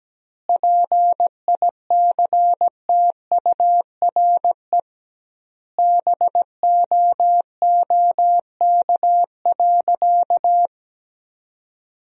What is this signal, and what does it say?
Morse code, heard as a keyed tone, about 17 wpm, 700 Hz. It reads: PICTURE BOOK.
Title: 2x Word-Spacing - Sentences from Top 500 Words - 17wpm